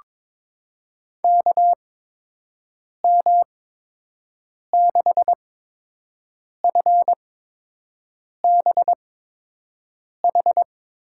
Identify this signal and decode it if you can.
Morse code, heard as a keyed tone, about 21 wpm, spaced as for 6 wpm, 700 Hz. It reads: KM6FBH
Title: USA Call Signs - 6 Characters - 8x Character Spacing - 22wpm